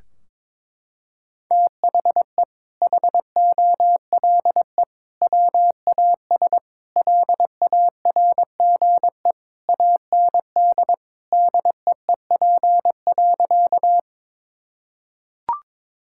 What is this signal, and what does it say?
Morse code, heard as a keyed tone, about 22 wpm, 700 Hz. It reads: THE HOLE WAS LARGE AND DEEP.
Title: Sentences from Top 1600 Words 22wpm